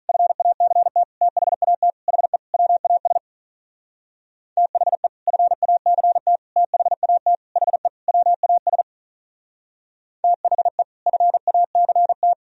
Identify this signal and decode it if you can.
Morse code, heard as a keyed tone, about 47 wpm, 700 Hz. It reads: FACT THAT HE WAS THE FACT THAT HE WAS THE FACT